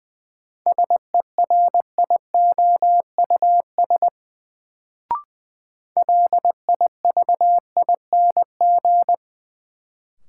Morse code, heard as a keyed tone, about 20 wpm, 700 Hz. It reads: SERIOUS LIVING
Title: Sentences from Top 1100 Words - New Word Drill Alt - 20wpm